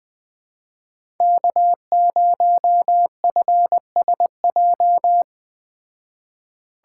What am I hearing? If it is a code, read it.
Morse code, heard as a keyed tone, about 20 wpm, 700 Hz. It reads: K0FSJ
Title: Speed Racing - USA 1x3 Call Signs 17wpm